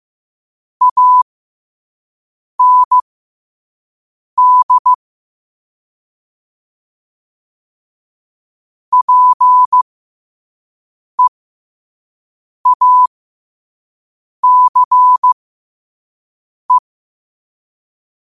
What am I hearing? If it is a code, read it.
Morse code, heard as a keyed tone, about 15 wpm, spaced as for 5 wpm, 1000 Hz. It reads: AND PEACE
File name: qotdfiles / 5wpm